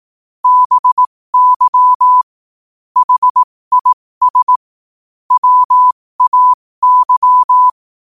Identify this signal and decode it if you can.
Morse code, heard as a keyed tone, about 18 wpm, spaced as for 15 wpm, 1000 Hz. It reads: BY HIS WAY
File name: qotdfiles / 15wpm